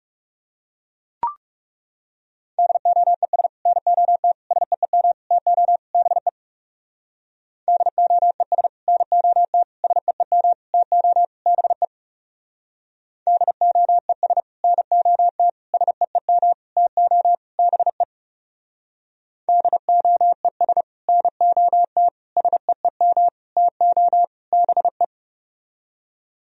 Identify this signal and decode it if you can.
Morse code, heard as a keyed tone, about 37 wpm, 700 Hz. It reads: DOES NOT SEEM TO BE DOES NOT SEEM TO BE DOES NOT SEEM TO BE DOES NOT SEEM TO BE
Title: Speed-Racing - Sets of 5 Words 30wpm